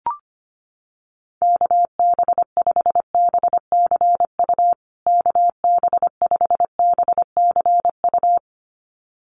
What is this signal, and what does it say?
Morse code, heard as a keyed tone, about 25 wpm, 700 Hz. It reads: KB5BCU KB5BCU
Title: USA Call Signs Repeated 2x 25wpm